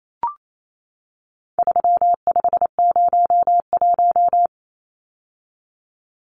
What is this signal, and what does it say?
Morse code, heard as a keyed tone, about 28 wpm, 700 Hz. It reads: 3501
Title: Numbers - 4-Digit 28wpm